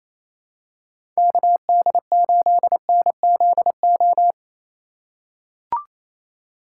Morse code, heard as a keyed tone, about 28 wpm, 700 Hz. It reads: KD8NZO
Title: USA Call Signs - 6 Characters - 3x Character Spacing - 28wpm